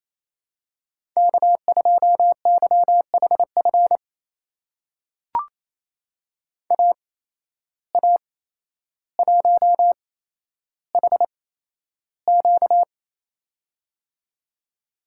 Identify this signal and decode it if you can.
Morse code, heard as a keyed tone, about 28 wpm, spaced as for 12 wpm, 700 Hz. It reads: K2YHF AA1HQ
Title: USA Call Signs - 5 Characters - 8x Character Spacing - 28wpm